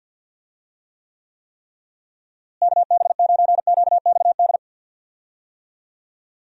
Morse code, heard as a keyed tone, about 50 wpm, 700 Hz. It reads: KB9QXD
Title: USA Call Signs - 6 Characters - Repeated 3x - 50wpm